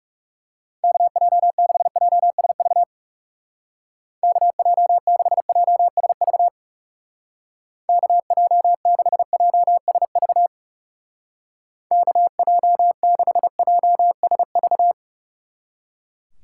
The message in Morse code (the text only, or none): KJ6JSV KJ6JSV KJ6JSV KJ6JSV